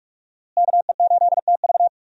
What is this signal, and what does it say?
Morse code, heard as a keyed tone, about 46 wpm, 700 Hz. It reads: KE8TV
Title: USA Call Signs - 5 Characters - Repeated 2x - 45wpm